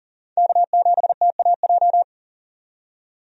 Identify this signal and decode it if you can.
Morse code, heard as a keyed tone, about 40 wpm, 700 Hz. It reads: K7TAJ
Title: USA Call Signs - 5 Characters - 40wpm